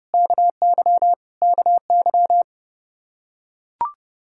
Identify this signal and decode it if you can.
Morse code, heard as a keyed tone, about 32 wpm, 700 Hz. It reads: KY KY
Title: K1USN SST QSO Practice 30wpm